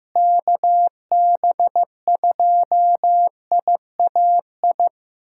Morse code, heard as a keyed tone, about 15 wpm, 700 Hz. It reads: KB2IAI
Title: USA Callsign DE Callsign - 6 Characters - 15wpm